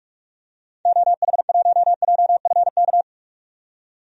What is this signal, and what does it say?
Morse code, heard as a keyed tone, about 45 wpm, 690 Hz. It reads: OH1JFK
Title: DXCC-100 Call Signs - 6 Characters - 45wpm